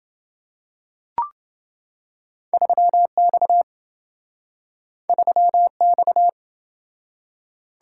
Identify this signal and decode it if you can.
Morse code, heard as a keyed tone, about 27 wpm, 700 Hz. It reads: 3X 3X